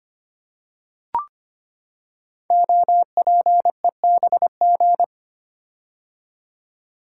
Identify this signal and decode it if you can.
Morse code, heard as a keyed tone, about 25 wpm, 700 Hz. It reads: OPEBG
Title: Code Groups - 5 Letter:Numbers - 25wpm